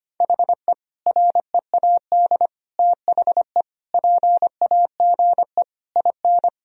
Morse code, heard as a keyed tone, about 25 wpm, 700 Hz. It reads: HE READ THE PAGE IN
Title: Sentences from Top 1000 Words - Review Part 1 - 25wpm